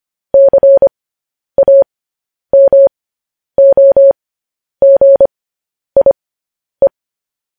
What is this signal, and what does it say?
Morse code, heard as a keyed tone, about 25 wpm, spaced as for 10 wpm, 550 Hz. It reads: CAMOGIE
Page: news-headlines-in-morse-code-at-10-wpm